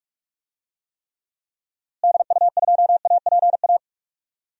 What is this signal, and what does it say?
Morse code, heard as a keyed tone, about 43 wpm, 700 Hz. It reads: DU2APA
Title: DXCC-100 Call Signs - 6 Characters - Repeated 3x - 45wpm